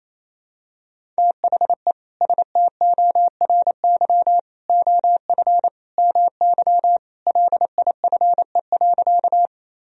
Morse code, heard as a keyed tone, about 28 wpm, 700 Hz. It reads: THE STORY OF MY LIFE.
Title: Sentences from Top 1000 Words - Review Part 1 - 28wpm